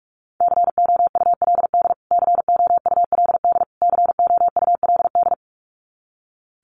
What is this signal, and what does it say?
Morse code, heard as a keyed tone, about 44 wpm, 700 Hz. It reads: COULD COULD COULD
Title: Top 300 Words Repeated 45wpm